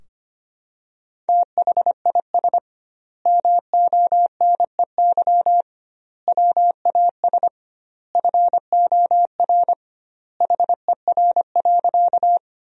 Morse code, heard as a keyed tone, about 25 wpm, 700 Hz. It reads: THIS MONEY WAS FOR HER.
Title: 2x Word-Spacing - Sentences from Top 400 Words - 25wpm